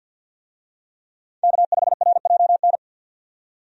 Morse code, heard as a keyed tone, about 46 wpm, 700 Hz. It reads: K5RJN
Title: USA Call Signs - 5 Characters - 50wpm